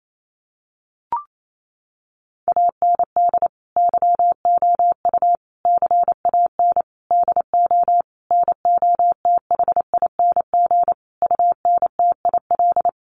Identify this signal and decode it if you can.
Morse code, heard as a keyed tone, about 28 wpm, 700 Hz. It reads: AND YOU CAN DO NOTHING UNTIL
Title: Sentences from Top 500 Words 28wpm